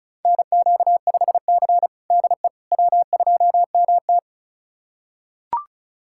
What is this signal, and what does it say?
Morse code, heard as a keyed tone, about 36 wpm, 700 Hz. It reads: NQ5C DE W2MT
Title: USA Callsign DE Callsign - 4 Characters - 35wpm